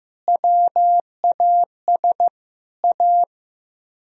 Morse code, heard as a keyed tone, about 16 wpm, 700 Hz. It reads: WAS A
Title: Speed-Racing - Sets of 5 Words 15wpm